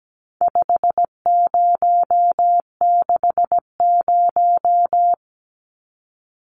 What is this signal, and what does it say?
Morse code, heard as a keyed tone, about 17 wpm, 700 Hz. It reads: 5060
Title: Numbers - 4-Digit 17wpm